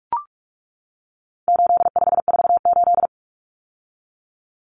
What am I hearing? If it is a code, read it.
Morse code, heard as a keyed tone, about 44 wpm, 700 Hz. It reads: 8548
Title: Numbers - 4-Digit 45wpm